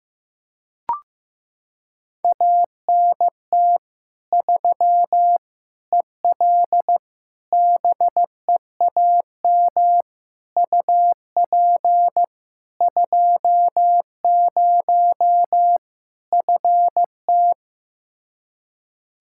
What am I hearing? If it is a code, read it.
Morse code, heard as a keyed tone, about 15 wpm, 700 Hz. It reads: ANT 3 EL BEAM UP 20 FT
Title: Practice QSO - Exchange 2 - Info - 15wpm